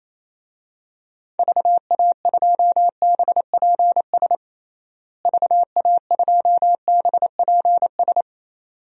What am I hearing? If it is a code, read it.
Morse code, heard as a keyed tone, about 28 wpm, 700 Hz. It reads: VA2BPS VA2BPS
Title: DXCC-100 Call Signs - 6 Characters - Repeated 2x - 28wpm